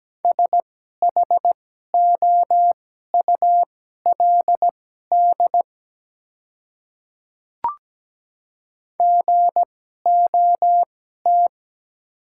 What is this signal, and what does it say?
Morse code, heard as a keyed tone, about 17 wpm, spaced as for 12 wpm, 700 Hz. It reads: SHOULD GOT
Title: 2x Character Spacing - Top 300 Words - 17wpm